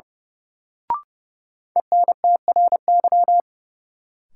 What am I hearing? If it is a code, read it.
Morse code, heard as a keyed tone, about 32 wpm, 700 Hz. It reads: ENTRY
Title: Sentences from Top 1500 Words - New Word Drill Alt - 30wpm